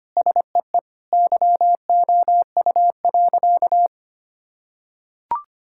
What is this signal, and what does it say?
Morse code, heard as a keyed tone, about 25 wpm, 700 Hz. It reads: SEE YOU.
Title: Sentences from Top 1000 Words - Review Part 5 - 25wpm